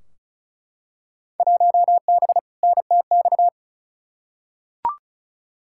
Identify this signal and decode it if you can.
Morse code, heard as a keyed tone, about 36 wpm, 700 Hz. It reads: 1B NTX
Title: Field Day Exchanges - 35wpm